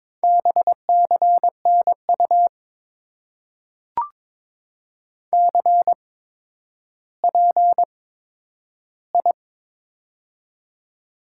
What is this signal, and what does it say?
Morse code, heard as a keyed tone, about 22 wpm, spaced as for 11 wpm, 700 Hz. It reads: BCNU CPI
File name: Top 90 QSO Elements - 8x Character Spacing - 22wpm